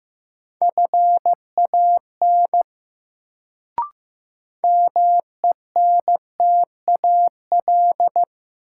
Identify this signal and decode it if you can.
Morse code, heard as a keyed tone, about 15 wpm, 700 Hz. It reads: FAN MENTAL
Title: Sentences from Top 1000 Words - Review Part 4 - New Word Drill - 15wpm